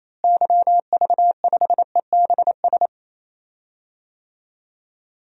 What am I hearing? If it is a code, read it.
Morse code, heard as a keyed tone, about 27 wpm, 700 Hz. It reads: YV5EBS
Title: DXCC-100 Call Signs - 6 Characters - Repeated 2x - 28wpm